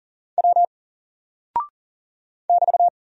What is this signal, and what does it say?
Morse code, heard as a keyed tone, about 40 wpm, 700 Hz. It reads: W <BT>
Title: Single Character (Letter:Number:Punctuation:Prosign) Alt 40wpm